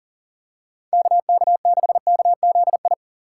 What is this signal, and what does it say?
Morse code, heard as a keyed tone, about 39 wpm, 690 Hz. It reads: KK6KZI
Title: USA Call Signs - 6x Character Spacing - 40wpm